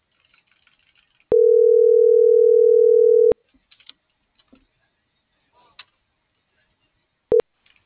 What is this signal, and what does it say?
Ringback tone (North America), 440 + 480 Hz.